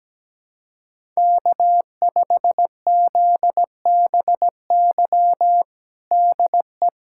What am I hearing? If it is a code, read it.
Morse code, heard as a keyed tone, about 17 wpm, 700 Hz. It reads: K5ZBY DE
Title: USA Callsign DE Callsign 17wpm